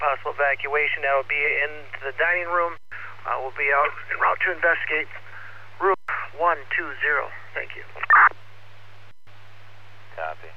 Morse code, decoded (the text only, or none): none